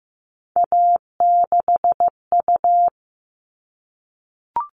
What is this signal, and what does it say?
Morse code, heard as a keyed tone, about 15 wpm, 700 Hz. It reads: A6U
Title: All Cut Numbers - 3-Digit 15wpm